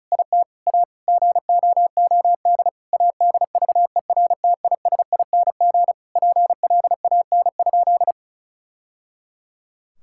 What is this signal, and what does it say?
Morse code, heard as a keyed tone, about 36 wpm, 700 Hz. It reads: IT A GOOD ADVERTISING PLAN?
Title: Sentences from Top 1000 Words - Review Part 6 - 35wpm